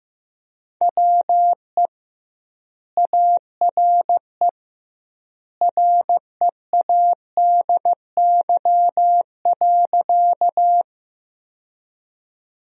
Morse code, heard as a keyed tone, about 15 wpm, 700 Hz. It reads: WE ARE READY.
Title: 2x Word-Spacing - Sentences from Top 400 Words - 15wpm